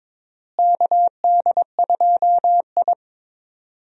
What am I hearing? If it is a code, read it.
Morse code, heard as a keyed tone, about 22 wpm, 700 Hz. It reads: KD2I
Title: USA Call Signs - 4 Characters - Repeated 2x - 22wpm